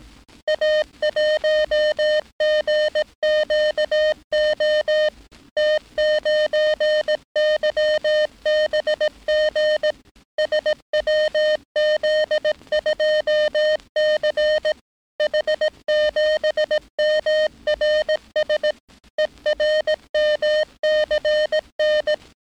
Morse code, heard as a keyed tone, about 18 wpm, 610 Hz. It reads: A1GQO T9YBG SWZ2C H7MRS ERMCN
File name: training / NAVPERS 11049to11060 / NAVPERS 11059 RA RB